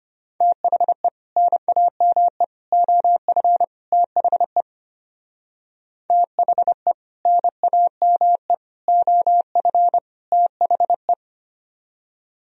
Morse code, heard as a keyed tone, about 33 wpm, 700 Hz. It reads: THE NAME OF THE THE NAME OF THE